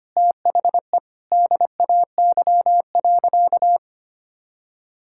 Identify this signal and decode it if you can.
Morse code, heard as a keyed tone, about 25 wpm, 700 Hz. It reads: THE DAY.
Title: Sentences from Top 1000 Words - Review Part 6 - 25wpm